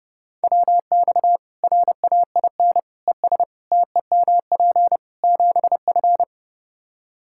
Morse code, heard as a keyed tone, about 30 wpm, 700 Hz. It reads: WX RAIN ES TEMP 7F
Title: Practice QSO - Exchange 2 - Info - 30wpm